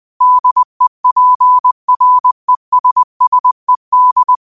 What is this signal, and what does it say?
Morse code, heard as a keyed tone, about 20 wpm, 1000 Hz. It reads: DEPRESSED
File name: qotdfiles / 20wpm